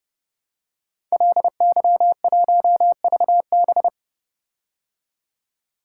Morse code, heard as a keyed tone, about 30 wpm, 700 Hz. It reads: LY1VB
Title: Code Groups - 5 Letter:Numbers - 30wpm